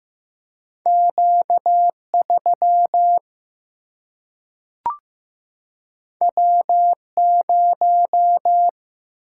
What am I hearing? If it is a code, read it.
Morse code, heard as a keyed tone, about 15 wpm, 700 Hz. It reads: Q 3 W 0
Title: Single-Letter Single-Number 15wpm